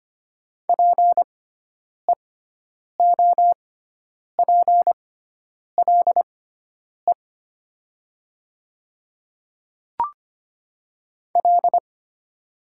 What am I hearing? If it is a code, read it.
Morse code, heard as a keyed tone, about 25 wpm, spaced as for 9 wpm, 700 Hz. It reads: PEOPLE L
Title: 6x Character Spacing - Top 500 Words - 25wpm